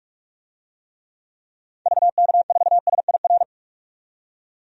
Morse code, heard as a keyed tone, about 46 wpm, 700 Hz. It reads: VK4SIR
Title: DXCC-100 Call Signs - 6 Characters - 45wpm